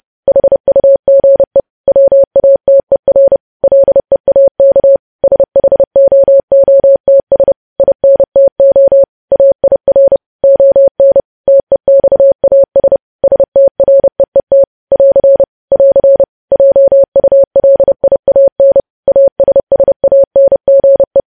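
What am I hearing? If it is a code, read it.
Morse code, heard as a keyed tone, about 30 wpm, 550 Hz. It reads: HUGE WATER LEAK SHOOTS INTO AIR ON TEXAS STREET <AR> <AR> JULIAN ASSANGE
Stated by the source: News headlines sent as Morse practice at 30.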